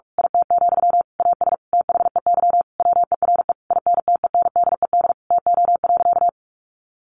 Morse code, heard as a keyed tone, about 44 wpm, 700 Hz. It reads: IT, AS THEY WERE INTENDED TO.
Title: A Sound Of Thunder 45wpm